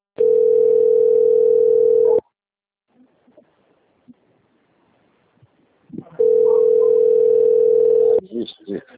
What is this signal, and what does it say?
Ringback tone (North America), 440 + 480 Hz, with speech.
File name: rg-1600-unknown- - - .13323